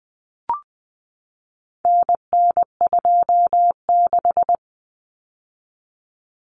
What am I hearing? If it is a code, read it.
Morse code, heard as a keyed tone, about 20 wpm, 700 Hz. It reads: NN26
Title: Common Cut Numbers - 4-Digit 20wpm